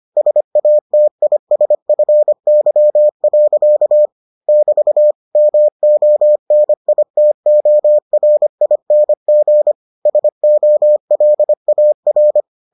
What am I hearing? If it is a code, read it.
Morse code, heard as a keyed tone, about 25 wpm, 600 Hz. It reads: SATISFY. <BT> MONITORING SOLAR